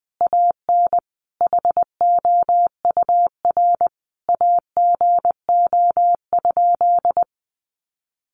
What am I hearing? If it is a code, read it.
Morse code, heard as a keyed tone, about 20 wpm, 700 Hz. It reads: AN HOUR AGO?
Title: Sentences from Top 1300 Words 20wpm